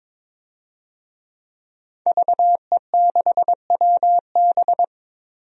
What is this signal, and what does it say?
Morse code, heard as a keyed tone, about 23 wpm, 700 Hz. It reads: VE6WB
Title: DXCC-100 Call Signs - 5 Characters - 22wpm